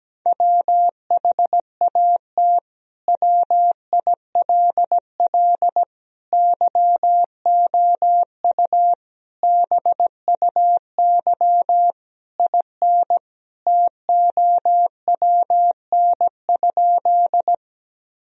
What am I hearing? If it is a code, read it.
Morse code, heard as a keyed tone, about 17 wpm, 700 Hz. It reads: WHAT WILL YOU BUY IN TOWN?
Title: Sentences from Top 1000 Words - Review Part 4 - 17wpm